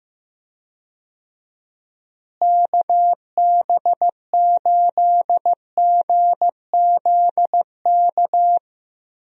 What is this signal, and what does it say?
Morse code, heard as a keyed tone, about 16 wpm, 700 Hz. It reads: KB8GZK
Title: USA Call Signs - 6 Characters - Repeated 2x - 15wpm